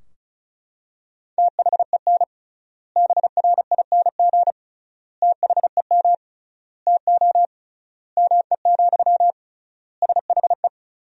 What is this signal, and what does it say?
Morse code, heard as a keyed tone, about 36 wpm, 700 Hz. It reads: THEN BRING THEM TO ME, SHE
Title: 3x Word-Spacing - Sentences from Top 500 Words - 35wpm